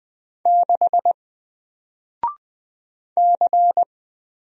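Morse code, heard as a keyed tone, about 20 wpm, 700 Hz. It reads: 6 C
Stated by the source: Single letter and number Morse drill at 20 wpm, from N4LCD.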